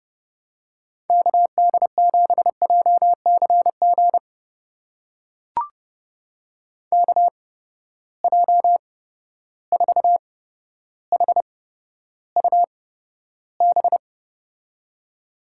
Morse code, heard as a keyed tone, about 31 wpm, spaced as for 13 wpm, 700 Hz. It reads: KD7JCG KJ4HUB
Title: USA Call Signs - 6 Characters - 8x Character Spacing - 30wpm